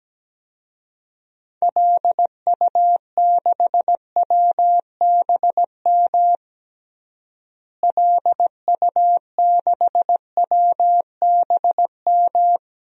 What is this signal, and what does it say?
Morse code, heard as a keyed tone, about 17 wpm, 700 Hz. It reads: LU6WBM LU6WBM
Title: DXCC-100 Call Signs - 6 Characters - Repeated 2x - 17wpm